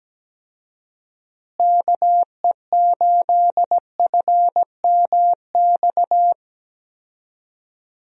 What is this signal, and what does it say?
Morse code, heard as a keyed tone, about 17 wpm, 700 Hz. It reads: KE8FMX
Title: USA Call Signs - 6 Characters - Repeated 3x - 17wpm